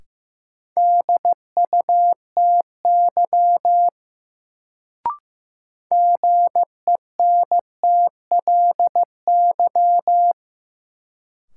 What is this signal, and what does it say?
Morse code, heard as a keyed tone, about 15 wpm, 700 Hz. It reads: DUTY GENTLY
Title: Sentences from Top 1500 Words - New Word Drill - 15wpm